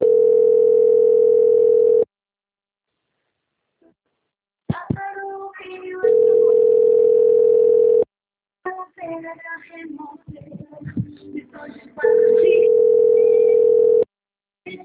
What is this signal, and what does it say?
Ringback tone (North America), 440 + 480 Hz.
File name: out- - - - - .7319